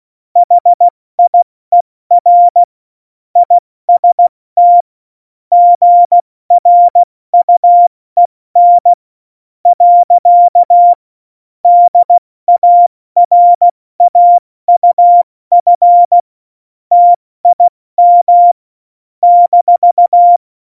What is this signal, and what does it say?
Morse code, heard as a keyed tone, about 16 wpm, spaced as for 14 wpm, 700 Hz. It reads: HIER IST GRUEN. DARAUF TIM -
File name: Morsekurs / Witze / Witz 21. August 2018, 70 F80